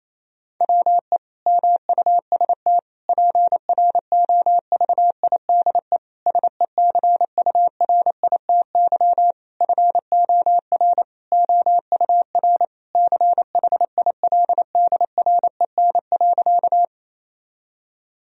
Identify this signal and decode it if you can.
Morse code, heard as a keyed tone, about 28 wpm, 700 Hz. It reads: WE MUST PROVIDE SECURITY FOR OUR CHILDREN.
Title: Sentences from Top 1000 Words - Review Part 4 - 28wpm